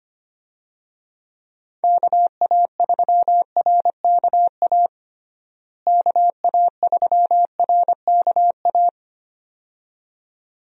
Morse code, heard as a keyed tone, about 25 wpm, 700 Hz. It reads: KA3RKA KA3RKA
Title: USA Call Signs - 6 Characters - Repeated 2x - 25wpm